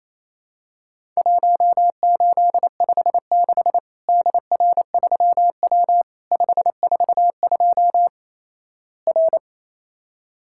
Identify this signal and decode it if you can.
Morse code, heard as a keyed tone, about 28 wpm, 700 Hz. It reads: 1856 DR3W 542 R
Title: Worked All Europe DX Contest - QTC Exchange - 28wpm